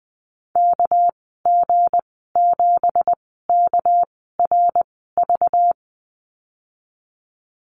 Morse code, heard as a keyed tone, about 20 wpm, spaced as for 15 wpm, 700 Hz. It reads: KG7KRV
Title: 231 Lesson 37g - Letter x - Call Signs 20wpm